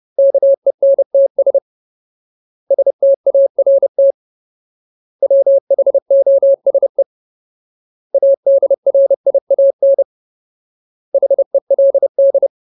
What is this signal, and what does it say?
Morse code, heard as a keyed tone, about 30 wpm, 550 Hz. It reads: KENTS START WHOSE ADRIAN HELD
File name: Ultimate Word List / 30WPM Ultimate